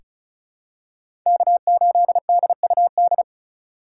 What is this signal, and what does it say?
Morse code, heard as a keyed tone, about 36 wpm, 700 Hz. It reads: K8DUD
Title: USA Call Signs - 5 Characters - 4x Character Spacing - 35wpm